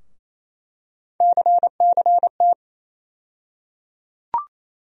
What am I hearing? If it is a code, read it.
Morse code, heard as a keyed tone, about 28 wpm, 700 Hz. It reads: CCT
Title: USA Call Sign Postfixes - 3 Letters 28wpm